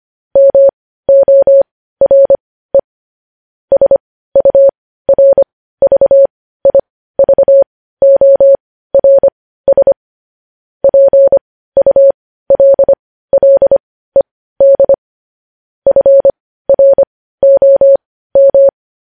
Morse code, heard as a keyed tone, about 25 wpm, spaced as for 15 wpm, 550 Hz. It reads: MORE SURVIVORS PULLED FROM
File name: podcast / News headlines-1x-25x15